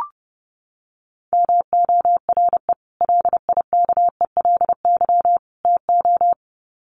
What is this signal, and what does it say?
Morse code, heard as a keyed tone, about 30 wpm, 700 Hz. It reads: MORE LIKELY TO
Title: Sets of 3 Words 30wpm